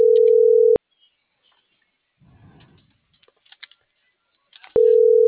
Ringback tone (North America), 440 + 480 Hz.